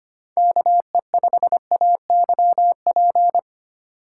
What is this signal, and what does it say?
Morse code, heard as a keyed tone, about 25 wpm, 700 Hz. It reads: KE5AYP
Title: USA Call Signs - 6x Character Spacing - 25wpm